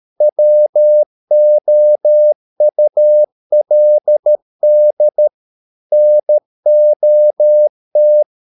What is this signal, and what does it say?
Morse code, heard as a keyed tone, about 13 wpm, 600 Hz. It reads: WOULD NOT